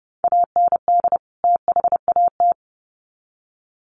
Morse code, heard as a keyed tone, about 30 wpm, 700 Hz. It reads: AND THAT
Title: Sets of 2 Words 30wpm